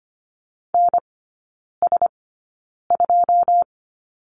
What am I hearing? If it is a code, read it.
Morse code, heard as a keyed tone, about 25 wpm, spaced as for 9 wpm, 700 Hz. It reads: NS2
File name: Lesson 14e - Number 2 - Three Characters 25wpm